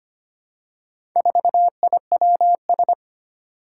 Morse code, heard as a keyed tone, about 25 wpm, 700 Hz. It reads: 4IWS
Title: Four Challenging Letter-Number Combinations 25wpm